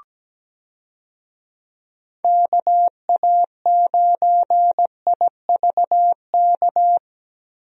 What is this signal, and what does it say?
Morse code, heard as a keyed tone, about 17 wpm, 700 Hz. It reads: KA9IVK
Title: USA Call Signs - 6 Characters - Repeated 2x - 17wpm